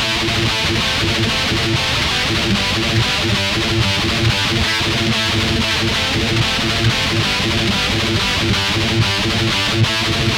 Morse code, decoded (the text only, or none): none